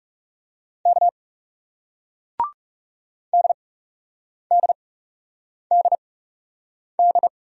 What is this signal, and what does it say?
Morse code, heard as a keyed tone, about 40 wpm, spaced as for 8 wpm, 700 Hz. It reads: K DDDD